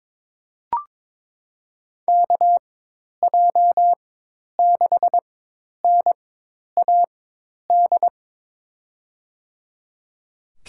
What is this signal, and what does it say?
Morse code, heard as a keyed tone, about 22 wpm, spaced as for 10 wpm, 700 Hz. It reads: KJ6NAD
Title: USA Call Signs - 4x Character Spacing - 22wpm